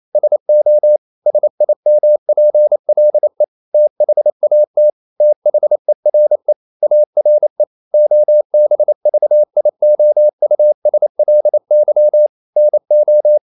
Morse code, heard as a keyed tone, about 28 wpm, 600 Hz. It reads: SO SIMPLE THAT THERE ARE OBVIOUSLY NO
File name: CW / Practice / EnglishQuotations0228